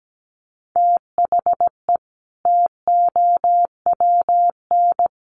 Morse code, heard as a keyed tone, about 17 wpm, 700 Hz. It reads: THE TOWN
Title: Sentences from Top 1300 Words 17wpm